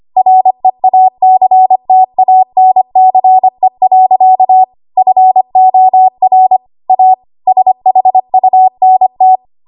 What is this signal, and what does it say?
Morse code, heard as a keyed tone, about 25 wpm, 750 Hz. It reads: REACTANCE. FOR A SHUNT
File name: Archive / 25 WPM / 25WPM